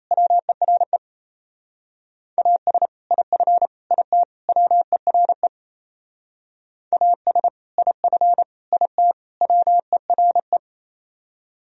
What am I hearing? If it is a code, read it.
Morse code, heard as a keyed tone, about 34 wpm, 700 Hz. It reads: WERE AS IF IT WERE AS IF IT WERE
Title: Speed-Racing - Sets of 4 Words 28wpm